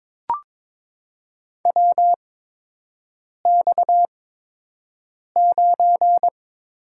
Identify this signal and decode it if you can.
Morse code, heard as a keyed tone, about 22 wpm, spaced as for 6 wpm, 700 Hz. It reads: WX9
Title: USA Call Signs - 4 Characters - 8x Character Spacing - 22wpm